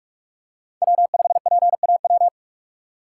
Morse code, heard as a keyed tone, about 42 wpm, 690 Hz. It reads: W5PAW
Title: USA Call Signs - 5 Characters - 2x Character Spacing - 45wpm